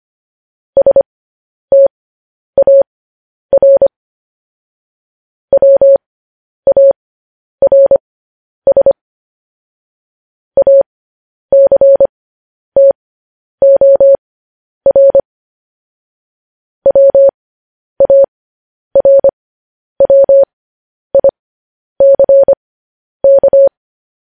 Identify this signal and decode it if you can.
Morse code, heard as a keyed tone, about 25 wpm, spaced as for 10 wpm, 550 Hz. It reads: STAR WARS ACTOR WARWICK